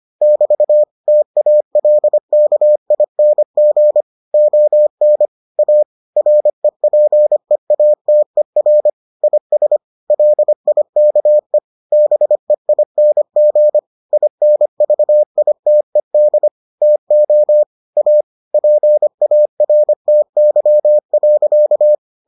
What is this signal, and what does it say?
Morse code, heard as a keyed tone, about 25 wpm, 600 Hz. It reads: <BT> TALKING ON A REPEATER IS LIKE BEING INVITED TO A PARTY.